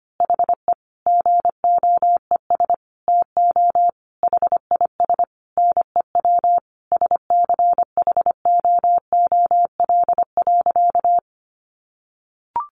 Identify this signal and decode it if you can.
Morse code, heard as a keyed tone, about 25 wpm, 700 Hz. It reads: HE GOES TO HIS NEW SCHOOL.
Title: Easy Phrases 25wpm